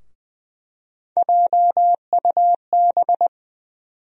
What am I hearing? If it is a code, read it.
Morse code, heard as a keyed tone, about 20 wpm, 700 Hz. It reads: JUB
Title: USA Call Sign Postfixes - 3 Letters 20wpm